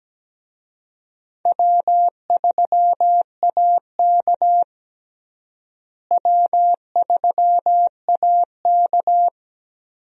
Morse code, heard as a keyed tone, about 17 wpm, 700 Hz. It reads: W3AK W3AK
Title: USA Call Signs - 4 Characters - Repeated 2x - 17wpm